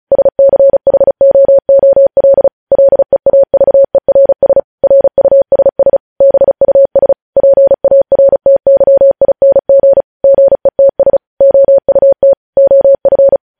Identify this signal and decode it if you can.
Morse code, heard as a keyed tone, about 36 wpm, 550 Hz. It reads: SCHOOL LEAVERS RUSS BUS PARTYING GETS OUT OF
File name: podcast / News headlines-1x-35x35